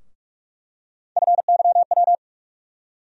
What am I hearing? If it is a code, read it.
Morse code, heard as a keyed tone, about 47 wpm, 700 Hz. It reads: FYW
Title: USA Call Sign Postfixes - 3 Letters 45wpm